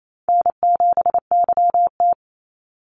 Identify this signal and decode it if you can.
Morse code, heard as a keyed tone, about 28 wpm, 700 Hz. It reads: N7YT
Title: Easy Call Signs - 28wpm